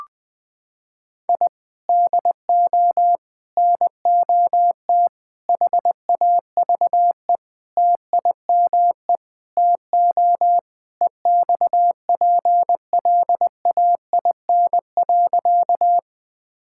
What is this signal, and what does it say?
Morse code, heard as a keyed tone, about 20 wpm, 700 Hz. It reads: I DO NOT HAVE TIME TO EXPLAIN.
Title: Sentences with 5-Word Phrases - 20wpm